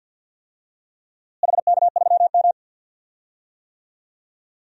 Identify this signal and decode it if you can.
Morse code, heard as a keyed tone, about 47 wpm, 700 Hz. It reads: HK3M
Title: DXCC-100 Call Signs - 4 Characters - Repeated 2x - 50wpm